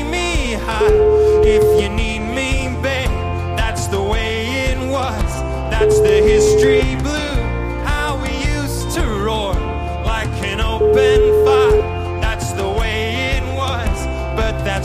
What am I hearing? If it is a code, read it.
Ringback tone (Europe, 425 Hz), with speech.